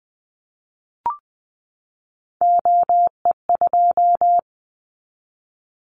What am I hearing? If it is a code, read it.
Morse code, heard as a keyed tone, about 19 wpm, 700 Hz. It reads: OE2
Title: Callsign Prefixes for Top 25 DXCC - 20wpm